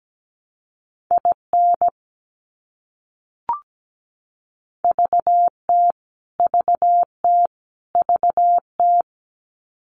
Morse code, heard as a keyed tone, about 17 wpm, 700 Hz. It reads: IN VT VT VT
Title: US States Repeated 3x 17wpm